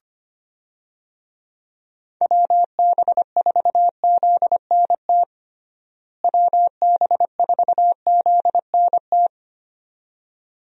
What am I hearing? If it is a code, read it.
Morse code, heard as a keyed tone, about 25 wpm, 700 Hz. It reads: WB4ZNT WB4ZNT
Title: USA Call Signs - 6 Characters - Repeated 2x - 25wpm